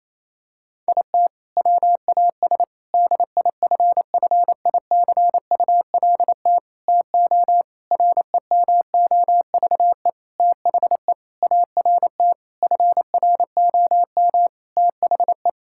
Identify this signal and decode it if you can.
Morse code, heard as a keyed tone, about 28 wpm, 700 Hz. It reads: IT WAS DIFFICULT TO REMOVE THE ART FROM THE
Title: Sentences from Top 1000 Words - Review Part 6 - 28wpm